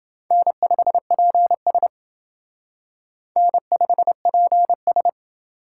Morse code, heard as a keyed tone, about 28 wpm, 700 Hz. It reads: N5PS N5PS